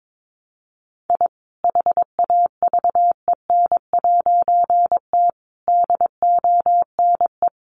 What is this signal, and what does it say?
Morse code, heard as a keyed tone, about 22 wpm, 700 Hz. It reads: I HAVEN'T DONE
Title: A Sound Of Thunder 22wpm